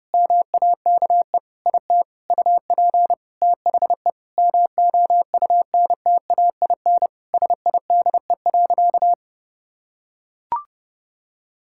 Morse code, heard as a keyed tone, about 32 wpm, 700 Hz. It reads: MAKE IT UP THE MOUNTAIN SIDE.
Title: Sentences from Top 1000 Words - Review Part 6 - 30wpm